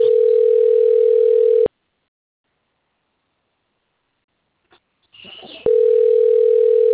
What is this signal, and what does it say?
Ringback tone (North America), 440 + 480 Hz.